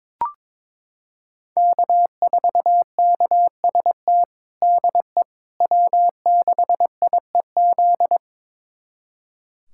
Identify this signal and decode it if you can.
Morse code, heard as a keyed tone, about 22 wpm, 700 Hz. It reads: K4KST DE W6IEZ
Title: USA Callsign DE Callsign - 5 Characters - 22wpm